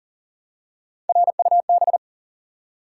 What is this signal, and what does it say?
Morse code, heard as a keyed tone, about 40 wpm, 700 Hz. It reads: RUB
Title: Three Letter Words - 40wpm